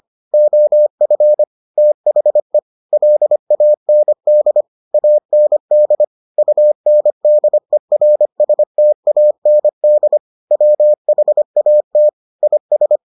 Morse code, heard as a keyed tone, about 25 wpm, 600 Hz. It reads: OF THE LAND AND UNDERSTAND WHAT IS